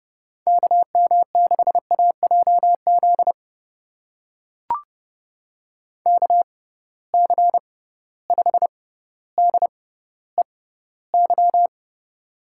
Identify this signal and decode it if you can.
Morse code, heard as a keyed tone, about 30 wpm, spaced as for 15 wpm, 700 Hz. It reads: KM6AJZ KC5DEY
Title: USA Call Signs - 6 Characters - 6x Character Spacing - 30wpm